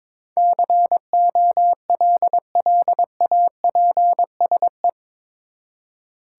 Morse code, heard as a keyed tone, about 22 wpm, 700 Hz. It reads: COLLAPSE
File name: practice / 22wpm / Eight Letter Words - 22wpm